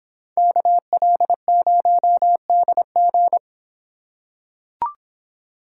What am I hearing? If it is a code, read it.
Morse code, heard as a keyed tone, about 25 wpm, 700 Hz. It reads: KL0DG